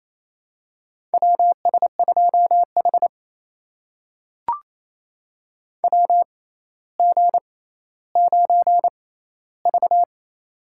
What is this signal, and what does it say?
Morse code, heard as a keyed tone, about 28 wpm, spaced as for 14 wpm, 700 Hz. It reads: WS2H WG9V
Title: USA Call Signs - 4 Characters - 6x Character Spacing - 28wpm